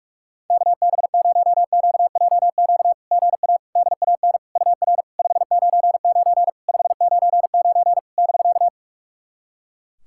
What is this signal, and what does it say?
Morse code, heard as a keyed tone, about 46 wpm, 690 Hz. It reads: KB0QJQ GA DAN UR 599 599 <BK>
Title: SOTA QSO Practice 45wpm